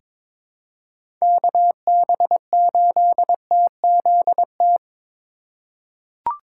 Morse code, heard as a keyed tone, about 22 wpm, 700 Hz. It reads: KB8TZT
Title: USA Call Signs - 4x Character Spacing - 22wpm